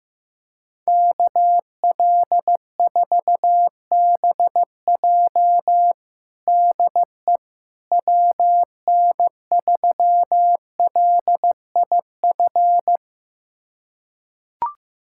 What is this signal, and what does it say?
Morse code, heard as a keyed tone, about 15 wpm, 700 Hz. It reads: KL4BJ DE WN3LIF
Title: USA Callsign DE Callsign 15wpm